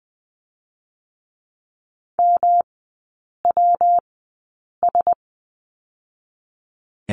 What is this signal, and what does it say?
Morse code, heard as a keyed tone, about 20 wpm, 700 Hz. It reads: M W S
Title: 133 Lesson 22e - Question Mark - Three Characters 20wpm